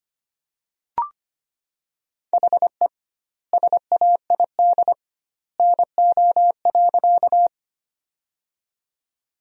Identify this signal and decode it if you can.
Morse code, heard as a keyed tone, about 25 wpm, 700 Hz. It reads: HE SAID NO.
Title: 2x Word-Spacing - Sentences from Top 100 Words - 25wpm